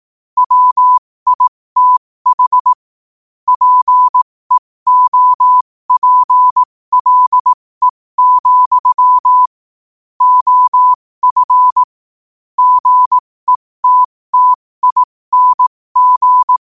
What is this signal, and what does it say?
Morse code, heard as a keyed tone, about 18 wpm, spaced as for 15 wpm, 1000 Hz. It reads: WITH PEOPLE, OF GETTING
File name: qotdfiles / 15wpm